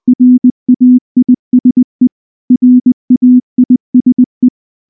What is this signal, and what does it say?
Morse code, heard as a keyed tone, about 19 wpm, 250 Hz. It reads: RAISE RAISE
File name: morse / morse-training-20WPM-8